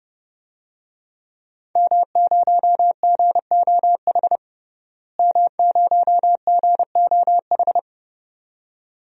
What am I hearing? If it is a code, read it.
Morse code, heard as a keyed tone, about 31 wpm, 700 Hz. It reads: M0GOH M0GOH
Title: DXCC-100 Call Signs - 5 Characters - Repeated 2x - 30wpm